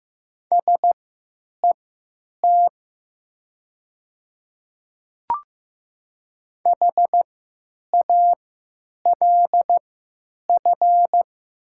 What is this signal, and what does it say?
Morse code, heard as a keyed tone, about 15 wpm, spaced as for 9 wpm, 700 Hz. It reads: SET HALF
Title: 3x Character Spacing - Top 500 Words - 15wpm